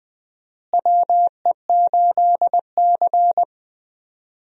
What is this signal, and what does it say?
Morse code, heard as a keyed tone, about 20 wpm, 700 Hz. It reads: WE8C
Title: USA Call Signs - 4 Characters - Repeated 2x - 20wpm